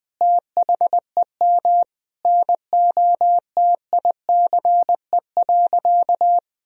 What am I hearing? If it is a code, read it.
Morse code, heard as a keyed tone, about 20 wpm, 700 Hz. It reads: THEM NOTICE.
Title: Sentences from Top 1000 Words - Review Part 2 - 20wpm